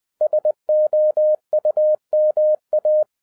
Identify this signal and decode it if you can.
Morse code, heard as a keyed tone, about 20 wpm, 610 Hz. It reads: SOUMA